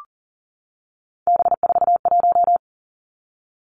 Morse code, heard as a keyed tone, about 42 wpm, 690 Hz. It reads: B41
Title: Three Challenging Letter-Number Combinations 40wpm